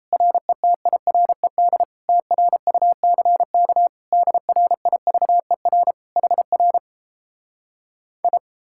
Morse code, heard as a keyed tone, about 37 wpm, 700 Hz. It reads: RETIRED TRUCK DRIVER HR I